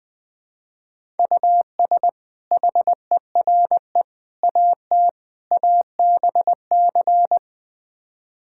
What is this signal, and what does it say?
Morse code, heard as a keyed tone, about 20 wpm, 700 Hz. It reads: US HERE AT ABC
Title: Build A Phrase - 5-Word Phrases - 20wpm